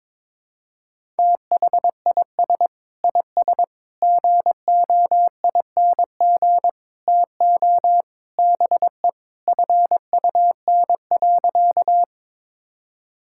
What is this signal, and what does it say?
Morse code, heard as a keyed tone, about 23 wpm, 700 Hz. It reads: THIS IS GOING TO BE FUN.
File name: practice / 22wpm / Sentences with 4-Word Phrases - 22wpm